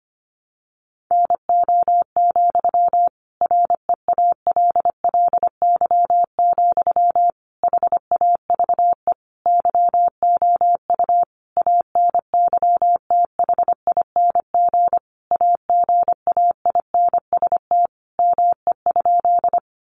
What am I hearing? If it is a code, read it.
Morse code, heard as a keyed tone, about 25 wpm, 700 Hz. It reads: NO, REALLY, HAVE YOU ANYTHING AGAINST ME?
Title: Sentences from Top 1000 Words 25wpm